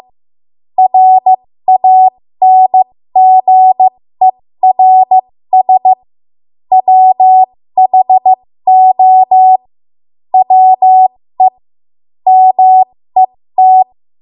Morse code, heard as a keyed tone, about 15 wpm, spaced as for 13 wpm, 750 Hz. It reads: RANGERS WHO WE MET
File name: Archive / 13 WPM / 13WPM